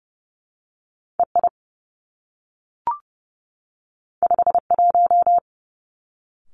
Morse code, heard as a keyed tone, about 30 wpm, 700 Hz. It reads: E I 5 1
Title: Two Challenging Letter-Number Combinations 30wpm